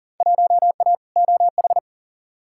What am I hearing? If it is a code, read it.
Morse code, heard as a keyed tone, about 40 wpm, 700 Hz. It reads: 1A OH
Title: Field Day Exchanges - 40wpm